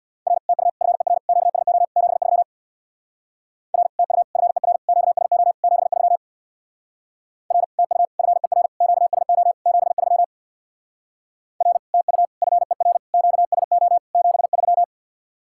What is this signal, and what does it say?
Morse code, heard as a keyed tone, about 65 wpm, 700 Hz. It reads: R TU FER QSO 73 R TU FER QSO 73 R TU FER QSO 73 R TU FER QSO 73